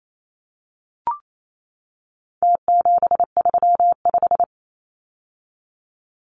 Morse code, heard as a keyed tone, about 28 wpm, 700 Hz. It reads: T735
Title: Numbers - Contest Style - 4-Digit 28wpm